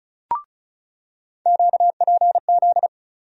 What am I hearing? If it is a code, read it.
Morse code, heard as a keyed tone, about 36 wpm, 700 Hz. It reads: QPZ
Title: USA Call Sign Postfixes - 3 Letters 35wpm